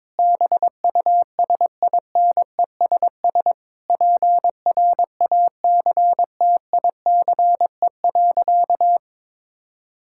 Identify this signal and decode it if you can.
Morse code, heard as a keyed tone, about 22 wpm, 700 Hz. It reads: BUSINESS PRACTICE.
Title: Sentences from Top 1000 Words - Review Part 6 - 22wpm